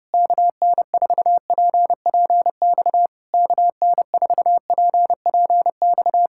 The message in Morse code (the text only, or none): KN4PPX KN4PPX